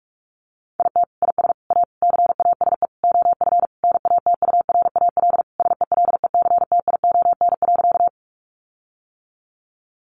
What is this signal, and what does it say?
Morse code, heard as a keyed tone, about 44 wpm, 700 Hz. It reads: IT IS A CASE OF NATURAL SELECTION.
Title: Sentences from Top 1400 Words 45wpm